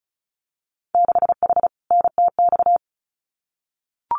Morse code, heard as a keyed tone, about 36 wpm, 700 Hz. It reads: 6H NTX
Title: Winter Field Day Exchanges Encore 35wpm